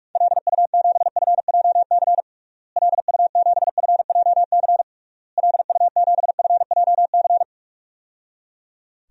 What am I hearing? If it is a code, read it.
Morse code, heard as a keyed tone, about 46 wpm, 700 Hz. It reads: LU7FJC LU7FJC LU7FJC
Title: DXCC-100 Call Signs - 6 Characters - Repeated 3x - 45wpm